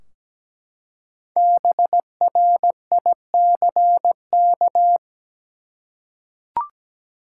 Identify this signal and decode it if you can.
Morse code, heard as a keyed tone, about 17 wpm, 700 Hz. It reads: BRICK
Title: Five Letter Words - 17wpm